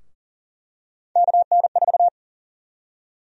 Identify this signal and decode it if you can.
Morse code, heard as a keyed tone, about 40 wpm, 700 Hz. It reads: KN4
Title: USA Call Sign Prefixes - 40wpm